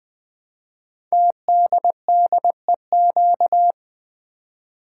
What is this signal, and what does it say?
Morse code, heard as a keyed tone, about 20 wpm, 700 Hz. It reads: TDDEQ
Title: Code Groups - 5 Letters - 20wpm